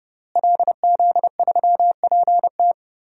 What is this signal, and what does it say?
Morse code, heard as a keyed tone, about 30 wpm, 700 Hz. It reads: LZ3PT
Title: DXCC-100 Call Signs - 5 Characters - Repeated 3x - 30wpm